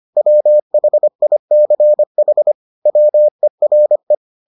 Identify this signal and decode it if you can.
Morse code, heard as a keyed tone, about 25 wpm, 600 Hz. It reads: WHICH WERE